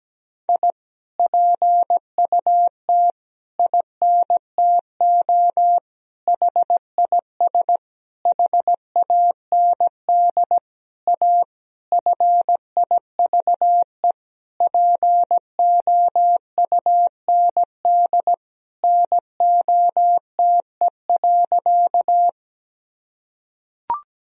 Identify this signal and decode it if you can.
Morse code, heard as a keyed tone, about 17 wpm, 700 Hz. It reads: I PUT INTO HIS HAND A FIVE POUND NOTE.
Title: Sentences from Top 1000 Words - Review Part 2 - 17wpm